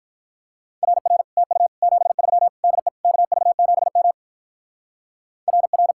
Morse code, heard as a keyed tone, about 43 wpm, 700 Hz. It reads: RR TU 73 DE KV7M RR
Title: Speed-Racing Common QSO Phrases - 35wpm